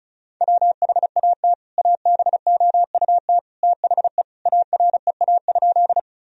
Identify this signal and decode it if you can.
Morse code, heard as a keyed tone, about 34 wpm, 700 Hz. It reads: WHAT ABOUT THE AREA?
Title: Sentences from Top 1000 Words - Review Part 2 - 35wpm